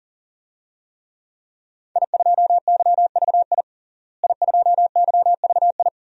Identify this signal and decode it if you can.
Morse code, heard as a keyed tone, about 40 wpm, 700 Hz. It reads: I2YVI I2YVI
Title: DXCC-100 Call Signs - 5 Characters - Repeated 3x - 40wpm